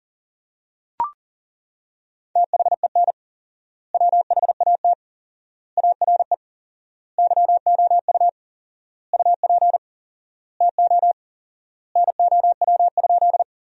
Morse code, heard as a keyed tone, about 40 wpm, 700 Hz. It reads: THEN WHAT ARE YOU UP TO NOW?
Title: 4x Word-Spacing - Sentences from Top 100 Words - 40wpm